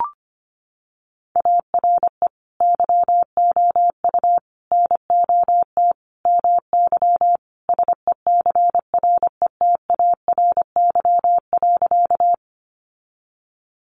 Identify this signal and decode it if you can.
Morse code, heard as a keyed tone, about 25 wpm, 700 Hz. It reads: ARE YOU NOT MY SECRETARY.
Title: Sentences from Top 1300 Words 25wpm